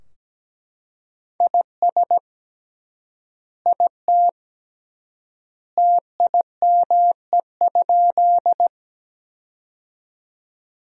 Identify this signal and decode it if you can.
Morse code, heard as a keyed tone, about 17 wpm, 700 Hz. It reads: IS IT TIME?
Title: 3x Word-Spacing - Sentences from Top 100 Words - 17wpm